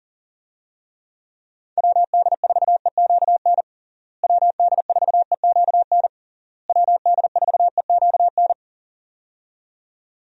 Morse code, heard as a keyed tone, about 40 wpm, 700 Hz. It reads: WD4EQN WD4EQN WD4EQN
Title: USA Call Signs - 6 Characters - Repeated 3x - 40wpm